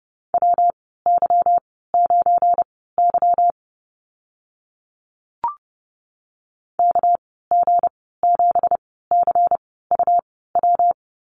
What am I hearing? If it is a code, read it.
Morse code, heard as a keyed tone, about 30 wpm, spaced as for 17 wpm, 700 Hz. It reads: WY9Y KG7CUW
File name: Lesson 37g - Letter x - Call Signs 30wpm